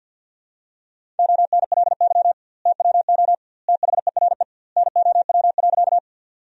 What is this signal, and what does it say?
Morse code, heard as a keyed tone, about 50 wpm, 700 Hz. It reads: ONLY TWO THERE NOW.
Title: 2x Word-Spacing - Sentences from Top 200 Words - 50wpm